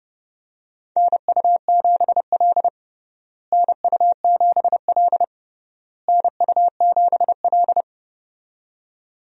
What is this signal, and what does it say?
Morse code, heard as a keyed tone, about 30 wpm, 700 Hz. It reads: NU7L NU7L NU7L
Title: USA Call Signs - 4 Characters - Repeated 3x - 30wpm